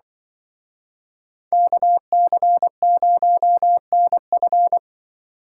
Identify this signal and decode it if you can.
Morse code, heard as a keyed tone, about 25 wpm, 700 Hz. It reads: KC0NF
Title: Speed Racing - USA 2x2 Call Signs 20wpm